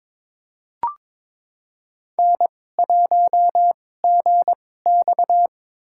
Morse code, heard as a keyed tone, about 22 wpm, spaced as for 16 wpm, 700 Hz. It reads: N1GX
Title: USA Call Signs - 4 Characters - 2x Character Spacing - 22wpm